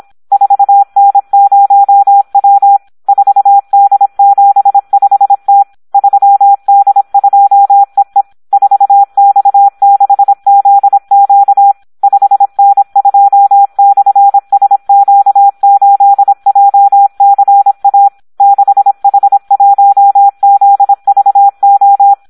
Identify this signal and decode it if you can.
Morse code, heard as a keyed tone, about 26 wpm, 800 Hz. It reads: 4N0W 4D75T 3D2EE 4X6ZQ 5N2/SQ8JCA 6H1ZVO
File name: Audio practica CW / SD 1 26